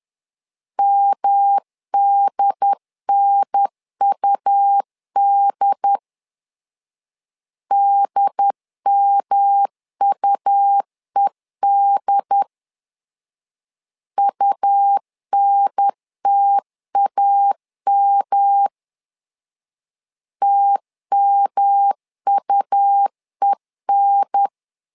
Morse code, heard as a keyed tone, about 11 wpm, 800 Hz. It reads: MDNUD DMUED UNTAM TMUEN
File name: LEZIONI / 3 MWRO / 3 M 10wpm 1828